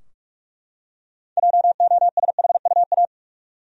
Morse code, heard as a keyed tone, about 44 wpm, 700 Hz. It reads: JOSHUA
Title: USA Common Names - 45wpm